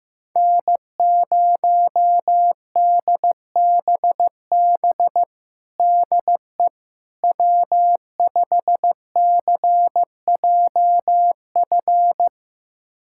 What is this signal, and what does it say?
Morse code, heard as a keyed tone, about 16 wpm, 700 Hz. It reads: N0DBB DE W5CJF
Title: USA Callsign DE Callsign - 5 Characters - 15wpm